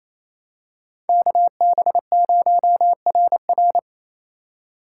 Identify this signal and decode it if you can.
Morse code, heard as a keyed tone, about 28 wpm, 700 Hz. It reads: KB0RR
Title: USA Call Signs - 2x Character Spacing - 28wpm